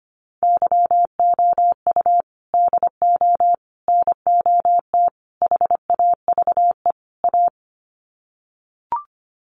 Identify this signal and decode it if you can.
Morse code, heard as a keyed tone, about 25 wpm, 700 Hz. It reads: YOU DO NOT HAVE A
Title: Sets of 5 Words 25wpm